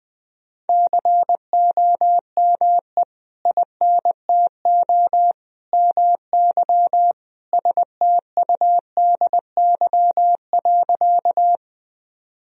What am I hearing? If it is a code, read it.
Morse code, heard as a keyed tone, about 20 wpm, 700 Hz. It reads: COME INTO MY STUDY.
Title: Sentences from Top 1000 Words - Review Part 1 - 20wpm